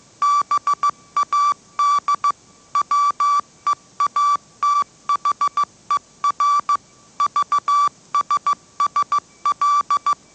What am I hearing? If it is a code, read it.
Morse code, heard as a keyed tone, about 15 wpm, 1160 Hz. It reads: BAD WEATHER VSSL
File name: mrx / mrx066 16wpm